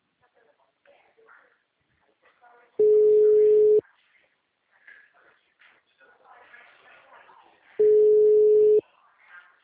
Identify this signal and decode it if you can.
Ringback tone (Europe, 425 Hz), with speech.